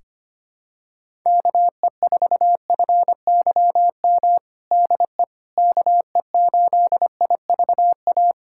Morse code, heard as a keyed tone, about 25 wpm, 700 Hz. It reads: KE4FYM DE KE8IVA
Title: USA Callsign DE Callsign - 6 Characters - 25wpm